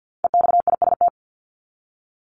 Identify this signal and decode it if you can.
Morse code, heard as a keyed tone, about 50 wpm, 700 Hz. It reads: EXIST
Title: ING Suffix - Top 100 Words with Base Words 50wpm